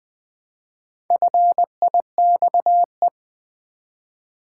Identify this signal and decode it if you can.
Morse code, heard as a keyed tone, about 20 wpm, 700 Hz. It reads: FIXE
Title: Build A Word - 1 Letter At A Time - 3 to 5-Letter Words - 20wpm